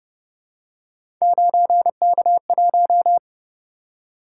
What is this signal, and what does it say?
Morse code, heard as a keyed tone, about 31 wpm, 700 Hz. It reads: 9K1
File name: Speed-Racing - Three Letter-Number Combinations - 25wpm